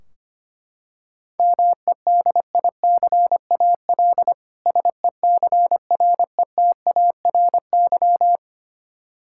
Morse code, heard as a keyed tone, about 25 wpm, 700 Hz. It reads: MEDICAL SECRETARY
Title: K1USN SST QSO Practice 25wpm